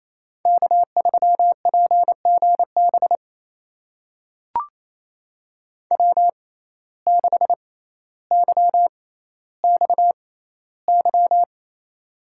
Morse code, heard as a keyed tone, about 28 wpm, spaced as for 14 wpm, 700 Hz. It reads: K3PGB W6YXY
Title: USA Call Signs - 5 Characters - 6x Character Spacing - 28wpm